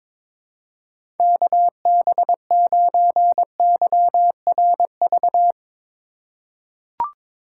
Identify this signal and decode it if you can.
Morse code, heard as a keyed tone, about 23 wpm, 700 Hz. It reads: KB9YRV
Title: USA Call Signs - 6 Characters - 3x Character Spacing - 22wpm